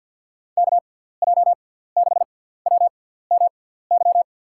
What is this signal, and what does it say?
Morse code, heard as a keyed tone, about 47 wpm, spaced as for 17 wpm, 700 Hz. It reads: KJ6WMY
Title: USA Call Signs - 6 Characters - 6x Character Spacing - 50wpm